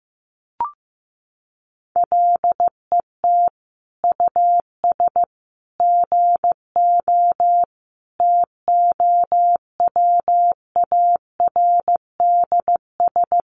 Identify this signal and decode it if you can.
Morse code, heard as a keyed tone, about 15 wpm, 700 Hz. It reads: LET US GO TOWARDS
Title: Sentences from Top 1100 Words 15wpm